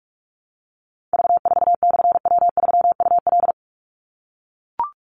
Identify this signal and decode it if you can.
Morse code, heard as a keyed tone, about 44 wpm, 700 Hz. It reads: V4/W3UL
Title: Call Signs - 45wpm